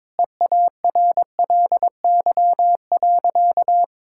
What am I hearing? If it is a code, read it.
Morse code, heard as a keyed tone, about 22 wpm, 700 Hz. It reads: EARLY.
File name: practice / 22wpm / Sentences with 2-Word Phrases - 22wpm